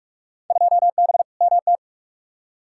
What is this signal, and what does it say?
Morse code, heard as a keyed tone, about 45 wpm, 700 Hz. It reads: 2B MT
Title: Field Day Exchanges Encore - 45wpm